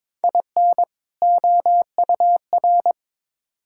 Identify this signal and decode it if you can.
Morse code, heard as a keyed tone, about 22 wpm, 700 Hz. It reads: IN OUR
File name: Build A Phrase - 2-Word Phrases - 22wpm